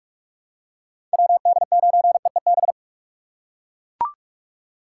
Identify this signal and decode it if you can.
Morse code, heard as a keyed tone, about 42 wpm, 700 Hz. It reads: WD9EEB
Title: USA Call Signs - 6 Characters - 6x Character Spacing - 45wpm